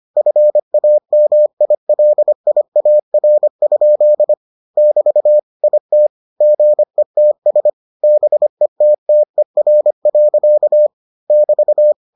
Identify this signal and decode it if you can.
Morse code, heard as a keyed tone, about 25 wpm, 600 Hz. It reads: FAMILIAR? <BT> IT GETS BETTER. <BT>